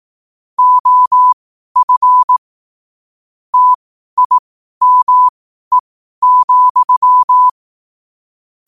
Morse code, heard as a keyed tone, about 18 wpm, spaced as for 12 wpm, 1000 Hz. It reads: OF TIME,
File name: qotdfiles / 12wpm